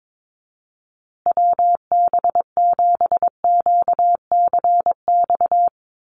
Morse code, heard as a keyed tone, about 22 wpm, 700 Hz. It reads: WB7QCX
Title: USA Call Signs Repeated 2x 22wpm